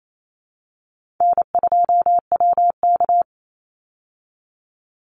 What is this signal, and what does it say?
Morse code, heard as a keyed tone, about 28 wpm, 700 Hz. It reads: N2WK
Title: Easy Call Signs Repeated 2x - 28wpm